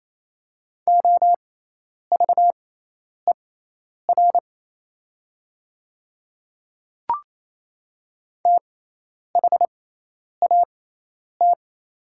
Morse code, heard as a keyed tone, about 27 wpm, spaced as for 10 wpm, 690 Hz. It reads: OVER THAT
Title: 6x Character Spacing - Top 100 Words - 28wpm